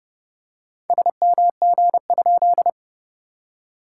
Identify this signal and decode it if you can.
Morse code, heard as a keyed tone, about 30 wpm, 700 Hz. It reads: SMG?
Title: SOTA QSO Practice 30wpm